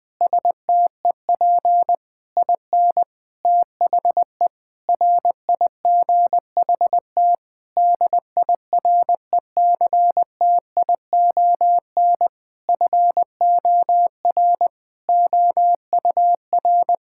Morse code, heard as a keyed tone, about 20 wpm, 700 Hz. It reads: STEP IN THE RIGHT DIRECTION FOR OUR
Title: Sentences with 8-Word Phrases - 20wpm